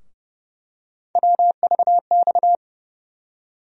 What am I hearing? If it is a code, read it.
Morse code, heard as a keyed tone, about 30 wpm, 700 Hz. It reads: WVX
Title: USA Call Sign Postfixes - 3 Letters 30wpm